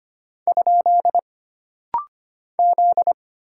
Morse code, heard as a keyed tone, about 25 wpm, 700 Hz. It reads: ? Z
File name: Single Character (Letter:Number:Punctuation:Prosign) - Rapid-Fire - 25wpm